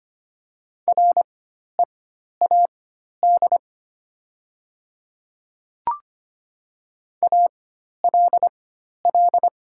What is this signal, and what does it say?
Morse code, heard as a keyed tone, about 25 wpm, spaced as for 12 wpm, 700 Hz. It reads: READ ALL
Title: 4x Character Spacing - Top 300 Words - 25wpm